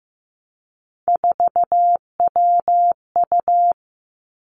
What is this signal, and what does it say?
Morse code, heard as a keyed tone, about 15 wpm, 700 Hz. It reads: 4WU
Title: Three Challenging Letter-Number Combinations 15wpm